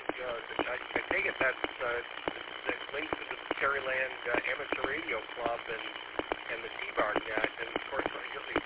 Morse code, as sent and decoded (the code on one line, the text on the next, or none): none